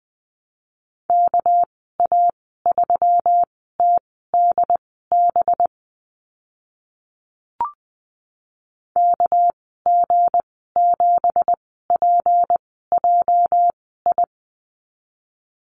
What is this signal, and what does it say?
Morse code, heard as a keyed tone, about 20 wpm, spaced as for 14 wpm, 700 Hz. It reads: KA3TDB KG7PJI
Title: 219 Lesson 35g - Number 0 - Call Signs 20wpm